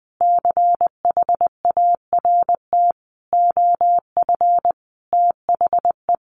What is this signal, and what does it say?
Morse code, heard as a keyed tone, about 20 wpm, 700 Hz. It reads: CHART OF THE
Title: Sentences from Top 1300 Words 20wpm